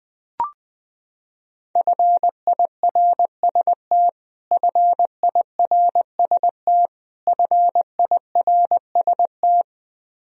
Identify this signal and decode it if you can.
Morse code, heard as a keyed tone, about 20 wpm, 700 Hz. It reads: FIRST FIRST FIRST
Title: Top 100 Words Repeated 20wpm